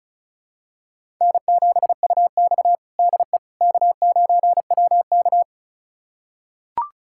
Morse code, heard as a keyed tone, about 36 wpm, 700 Hz. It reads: N7UX DE K9WK
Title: USA Callsign DE Callsign - 4 Characters - 35wpm